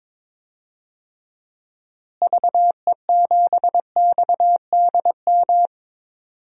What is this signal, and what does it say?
Morse code, heard as a keyed tone, about 22 wpm, 700 Hz. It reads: VE7XDM
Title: DXCC-100 Call Signs - 6 Characters - Repeated 3x - 22wpm